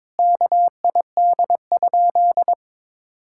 Morse code, heard as a keyed tone, about 21 wpm, 700 Hz. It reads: KID?
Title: Sentences from Top 1600 Words 22wpm